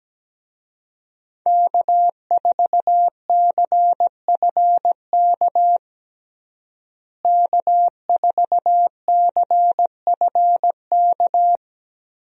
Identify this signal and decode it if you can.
Morse code, heard as a keyed tone, about 17 wpm, 700 Hz. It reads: K4CFK K4CFK
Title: USA Call Signs - 5 Characters - Repeated 3x - 17wpm